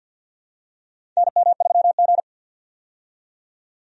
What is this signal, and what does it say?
Morse code, heard as a keyed tone, about 52 wpm, 700 Hz. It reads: NM3G
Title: USA Call Signs - 4 Characters - 50wpm